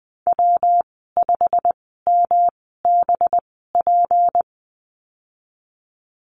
Morse code, heard as a keyed tone, about 20 wpm, spaced as for 15 wpm, 700 Hz. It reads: W5MBP
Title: 193 Lesson 31g - Letter v - Call Signs 20wpm